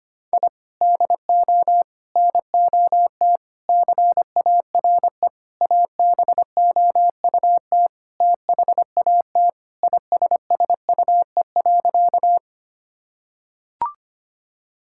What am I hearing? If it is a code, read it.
Morse code, heard as a keyed tone, about 25 wpm, 700 Hz. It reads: I DO NOT CARE ABOUT THAT ISSUE.
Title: Sentences from Top 1000 Words - Review Part 6 - 25wpm